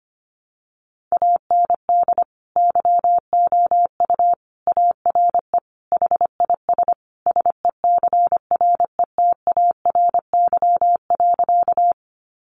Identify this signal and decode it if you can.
Morse code, heard as a keyed tone, about 25 wpm, 700 Hz. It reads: AND YOU ARE HIS SECRETARY.
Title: Sentences from Top 1300 Words 25wpm